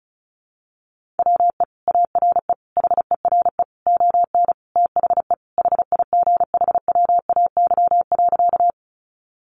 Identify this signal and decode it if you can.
Morse code, heard as a keyed tone, about 34 wpm, 700 Hz. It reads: WE ARE HERE ON THE HIGHWAY.
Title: Sentences from Top 1400 Words 35wpm